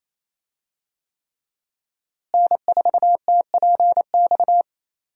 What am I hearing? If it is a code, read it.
Morse code, heard as a keyed tone, about 27 wpm, 700 Hz. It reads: N4TPX
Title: USA Call Signs - 5 Characters - Repeated 3x - 28wpm